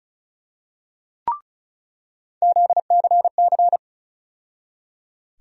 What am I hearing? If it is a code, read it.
Morse code, heard as a keyed tone, about 36 wpm, 700 Hz. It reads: ZCC
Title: USA Call Sign Postfixes - 3 Letters 35wpm